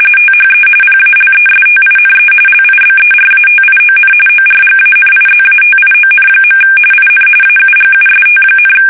RTTY, about 45.45 baud. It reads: THE PALESTINE LIBERATION ORGANIZATION. IN AUGUST,
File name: fsk850 / battleshipnj